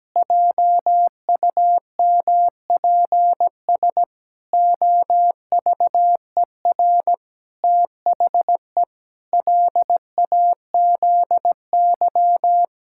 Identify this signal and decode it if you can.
Morse code, heard as a keyed tone, about 17 wpm, 700 Hz. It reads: JUMPS OVER THE LAZY
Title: Pangram Challenge v2 - 17wpm